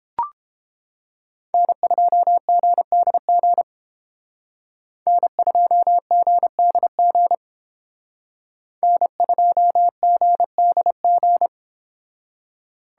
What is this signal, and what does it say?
Morse code, heard as a keyed tone, about 28 wpm, 700 Hz. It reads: N2GDG N2GDG N2GDG